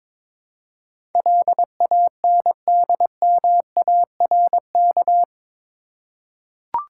Morse code, heard as a keyed tone, about 22 wpm, 700 Hz. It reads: LANDMARK
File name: practice / 22wpm / Eight Letter Words - 22wpm